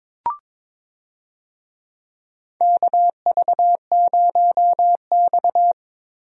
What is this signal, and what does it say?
Morse code, heard as a keyed tone, about 22 wpm, 700 Hz. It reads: KV0X
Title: USA Call Signs - 4 Characters - Repeated 2x - 22wpm